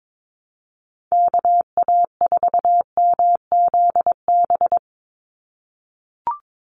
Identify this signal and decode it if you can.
Morse code, heard as a keyed tone, about 22 wpm, 700 Hz. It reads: KA4MZB
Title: USA Call Signs 22wpm